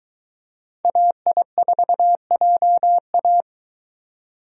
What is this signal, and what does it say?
Morse code, heard as a keyed tone, about 22 wpm, 700 Hz. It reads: AI4JA